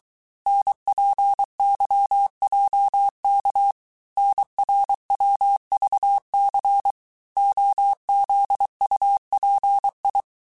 Morse code, heard as a keyed tone, about 23 wpm, 800 Hz. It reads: NPYJK NRWVC OZUPI
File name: cours-du-mercredi / audios / cours-855-texte-chiffre-vitesse-22-mpm